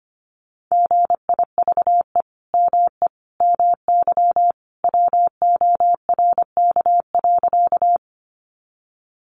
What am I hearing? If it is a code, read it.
Morse code, heard as a keyed tone, about 25 wpm, 700 Hz. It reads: GIVE ME MY WORK.
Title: Sentences from Top 200 Words 25wpm